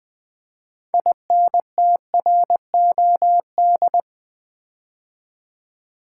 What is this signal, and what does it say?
Morse code, heard as a keyed tone, about 20 wpm, 700 Hz. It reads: INTROD
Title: Build A Word - 3 Letters At A Time - 9-Letter Words - 20wpm